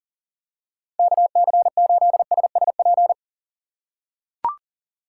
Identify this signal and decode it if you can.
Morse code, heard as a keyed tone, about 40 wpm, 700 Hz. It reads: KC8SSP
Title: USA Call Signs - 6 Characters - 2x Character Spacing - 40wpm